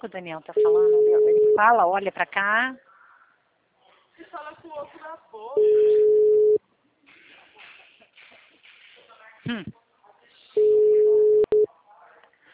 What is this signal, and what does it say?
Ringback tone (Europe, 425 Hz), with speech.
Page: TLML PRUDENTE